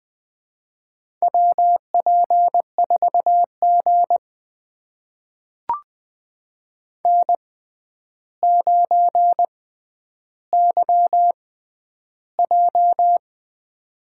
Morse code, heard as a keyed tone, about 20 wpm, spaced as for 10 wpm, 700 Hz. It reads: WP4G N9YJ
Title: USA Call Signs - 4 Characters - 6x Character Spacing - 20wpm